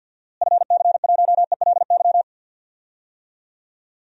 Morse code, heard as a keyed tone, about 52 wpm, 700 Hz. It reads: FC1ELY
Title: DXCC-100 Call Signs - 6 Characters - Repeated 2x - 50wpm